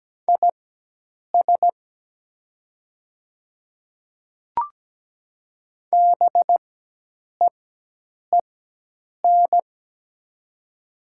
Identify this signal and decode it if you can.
Morse code, heard as a keyed tone, about 17 wpm, spaced as for 8 wpm, 700 Hz. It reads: IS BEEN
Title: 4x Character Spacing - Top 100 Words - 17wpm